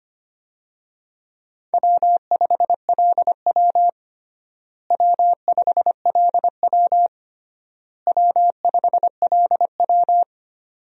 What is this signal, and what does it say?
Morse code, heard as a keyed tone, about 25 wpm, 700 Hz. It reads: W5LW W5LW W5LW
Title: USA Call Signs - 4 Characters - Repeated 3x - 25wpm